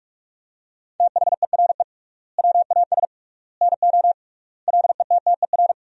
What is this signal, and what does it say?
Morse code, heard as a keyed tone, about 42 wpm, 700 Hz. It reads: THERE WAS NO LETTER
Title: 3x Word-Spacing - Sentences from Top 300 Words - 45wpm